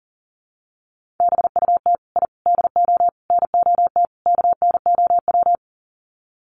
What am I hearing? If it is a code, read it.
Morse code, heard as a keyed tone, about 40 wpm, 700 Hz. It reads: BUT I DO NOT KNOW
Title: Sets of 5 Words 40wpm